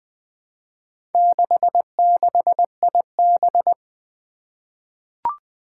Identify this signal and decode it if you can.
Morse code, heard as a keyed tone, about 20 wpm, 700 Hz. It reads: 66IB
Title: Four Challenging Letter-Number Combinations 20wpm